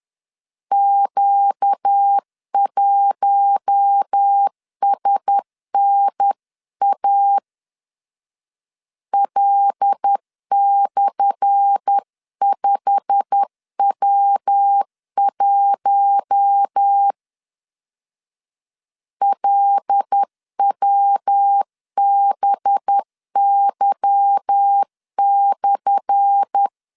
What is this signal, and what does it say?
Morse code, heard as a keyed tone, about 10 wpm, 800 Hz. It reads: Q1SNA L/5W1 LWBY/
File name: LEZIONI / 10 SIGNS / 10A SLASH 10wpm 2808